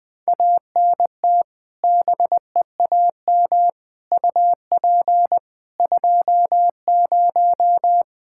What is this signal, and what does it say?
Morse code, heard as a keyed tone, about 20 wpm, 700 Hz. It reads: ANT BEAM UP 20
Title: K1USN SST QSO Practice 20wpm